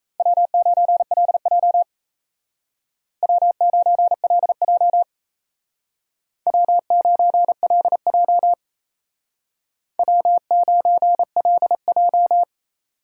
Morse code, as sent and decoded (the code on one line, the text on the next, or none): .-- ----. .-.. .--- / .-- ----. .-.. .--- / .-- ----. .-.. .--- / .-- ----. .-.. .---
W9LJ W9LJ W9LJ W9LJ